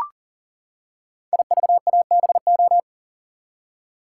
Morse code, heard as a keyed tone, about 40 wpm, 700 Hz. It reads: IVABO
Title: Code Groups - 5 Letters - 40wpm